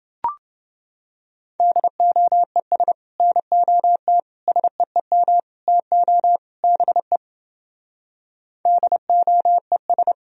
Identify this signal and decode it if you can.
Morse code, heard as a keyed tone, about 29 wpm, 700 Hz. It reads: DOES NOT SEEM TO BE DOES